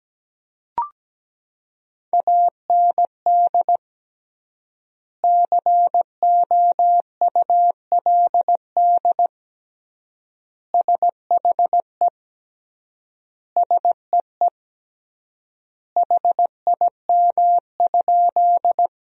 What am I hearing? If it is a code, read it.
Morse code, heard as a keyed tone, about 17 wpm, 700 Hz. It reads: AND COULD SHE SEE HIM?
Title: 3x Word-Spacing - Sentences from Top 100 Words - 17wpm